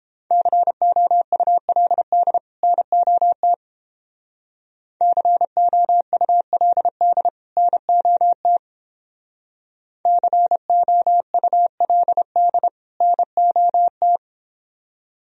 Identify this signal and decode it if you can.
Morse code, heard as a keyed tone, about 31 wpm, 700 Hz. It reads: COULD NOT COULD NOT COULD NOT